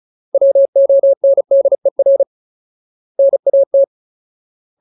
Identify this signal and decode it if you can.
Morse code, heard as a keyed tone, about 34 wpm, 550 Hz. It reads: WONDER NAT
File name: Ultimate Word List / 35WPM Ultimate